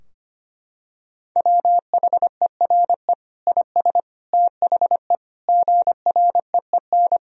Morse code, heard as a keyed tone, about 25 wpm, 700 Hz. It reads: WHERE IS THE GREEN
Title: Sentences from Top 1000 Words - Review Part 2 - 25wpm